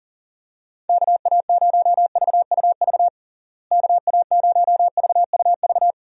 Morse code, heard as a keyed tone, about 37 wpm, 700 Hz. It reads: KA0VUV KA0VUV
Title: USA Call Signs - 6 Characters - Repeated 2x - 40wpm